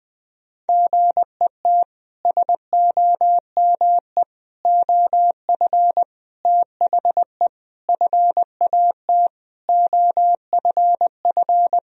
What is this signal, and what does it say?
Morse code, heard as a keyed tone, about 21 wpm, 700 Hz. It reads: GET SOME OF THE FAT OFF
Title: Sentences from Top 1000 Words - Review Part 4 - 20wpm